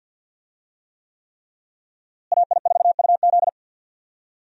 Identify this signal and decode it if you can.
Morse code, heard as a keyed tone, about 52 wpm, 700 Hz. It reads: AI4UZ
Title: USA Call Signs - 5 Characters - Repeated 3x - 50wpm